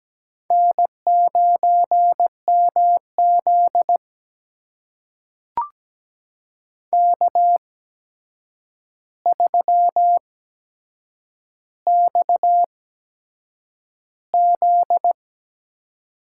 Morse code, heard as a keyed tone, about 17 wpm, spaced as for 7 wpm, 700 Hz. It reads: N9MZ K3XZ
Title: USA Call Signs - 4 Characters - 8x Character Spacing - 17wpm